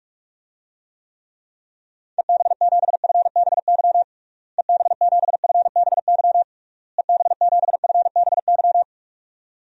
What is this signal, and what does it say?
Morse code, heard as a keyed tone, about 46 wpm, 700 Hz. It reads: EB7FBY EB7FBY EB7FBY
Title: DXCC-100 Call Signs - 6 Characters - Repeated 3x - 45wpm